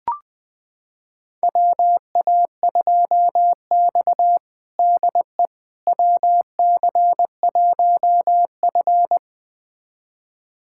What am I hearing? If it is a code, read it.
Morse code, heard as a keyed tone, about 20 wpm, 700 Hz. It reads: WA2X DE WC1F
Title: USA Callsign DE Callsign - 4 Characters - 20wpm